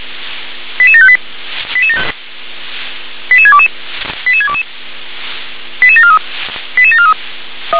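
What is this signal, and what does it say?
Five-tone selective call (ZVEI), heard as quick runs of five tones, tones of about 70 ms. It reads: 88758 88622 88622 88743 88743
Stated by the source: Received by SDR, ZVEI five-tone call 88730.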